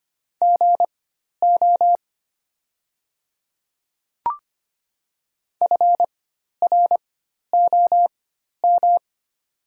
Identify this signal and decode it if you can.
Morse code, heard as a keyed tone, about 25 wpm, spaced as for 12 wpm, 690 Hz. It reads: GO FROM
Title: 4x Character Spacing - Top 200 Words - 25wpm